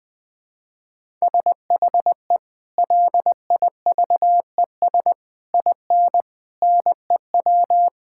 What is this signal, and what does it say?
Morse code, heard as a keyed tone, about 20 wpm, 700 Hz. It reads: SHE LIVES IN NEW
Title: Sentences with 3-Word Phrases - Spoken Beginning - 20wpm